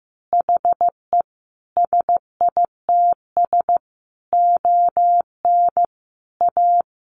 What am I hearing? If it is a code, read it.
Morse code, heard as a keyed tone, about 15 wpm, 700 Hz. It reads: HE SITS ON A
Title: Additional Easy Phrases 15wpm